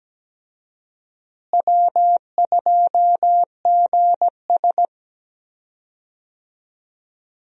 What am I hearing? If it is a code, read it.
Morse code, heard as a keyed tone, about 17 wpm, 700 Hz. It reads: W2GS
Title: USA Call Signs - 4 Characters - Repeated 2x - 17wpm